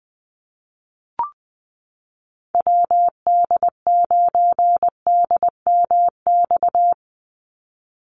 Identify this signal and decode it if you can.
Morse code, heard as a keyed tone, about 20 wpm, 700 Hz. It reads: WD9DMX
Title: USA Call Signs 20wpm